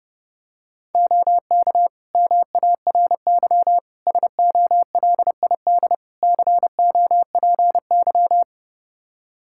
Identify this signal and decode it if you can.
Morse code, heard as a keyed tone, about 30 wpm, 700 Hz. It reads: OK MARY SOLID COPY
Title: Practice QSO - Exchange 3 - More Info - 30wpm